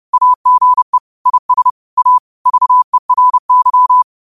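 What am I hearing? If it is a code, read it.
Morse code, heard as a keyed tone, about 30 wpm, 1000 Hz. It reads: AGE IS A VERY
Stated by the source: Morse quote of the day at 30 WPM.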